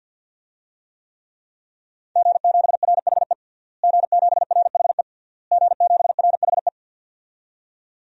Morse code, heard as a keyed tone, about 48 wpm, 700 Hz. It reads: G7RHE G7RHE G7RHE
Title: DXCC-100 Call Signs - 5 Characters - Repeated 3x - 50wpm